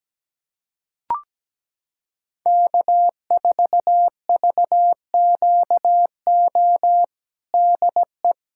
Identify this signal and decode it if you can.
Morse code, heard as a keyed tone, about 17 wpm, 700 Hz. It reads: K4VQO DE
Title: USA Callsign DE Callsign - 5 Characters - 17wpm